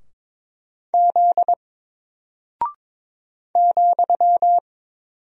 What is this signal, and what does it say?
Morse code, heard as a keyed tone, about 22 wpm, 700 Hz. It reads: Z ,
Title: Single Character (Letter:Number:Punctuation:Prosign) Alt 22wpm